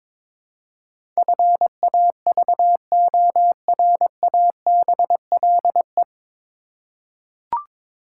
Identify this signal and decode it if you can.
Morse code, heard as a keyed tone, about 22 wpm, 700 Hz. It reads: FAVORABLE
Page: Morse Code Ninja practice 22wpm